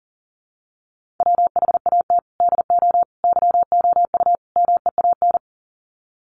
Morse code, heard as a keyed tone, about 40 wpm, 700 Hz. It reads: WHAT DO YOU MEAN
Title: Sets of 4 Words 40wpm